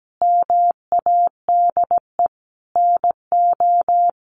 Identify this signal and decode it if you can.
Morse code, heard as a keyed tone, about 17 wpm, 700 Hz. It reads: MADE NO
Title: Sentences from Top 1400 Words 17wpm